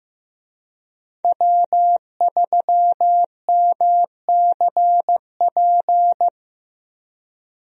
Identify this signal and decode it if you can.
Morse code, heard as a keyed tone, about 15 wpm, 700 Hz. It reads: W3MCP
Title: USA Call Signs - 5 Characters - Repeated 3x - 15wpm